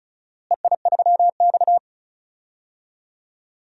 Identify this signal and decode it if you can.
Morse code, heard as a keyed tone, about 34 wpm, 700 Hz. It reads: EI3X
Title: DXCC-100 Call Signs - 4 Characters - Repeated 2x - 35wpm